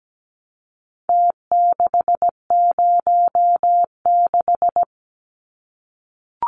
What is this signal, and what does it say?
Morse code, heard as a keyed tone, about 17 wpm, 700 Hz. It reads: T606
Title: Numbers - Contest Style - 4-Digit 17wpm